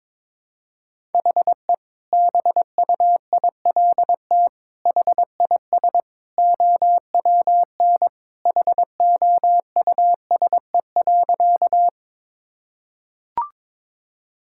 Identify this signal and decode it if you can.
Morse code, heard as a keyed tone, about 22 wpm, 700 Hz. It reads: HE BUILT HIS OWN HOUSE.
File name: practice / 22wpm / Sentences with 2-Word Phrases - 22wpm